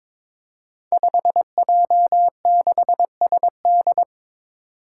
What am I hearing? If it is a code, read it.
Morse code, heard as a keyed tone, about 21 wpm, 700 Hz. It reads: 5J6SD
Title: Five Challenging Letter-Number Combinations 22wpm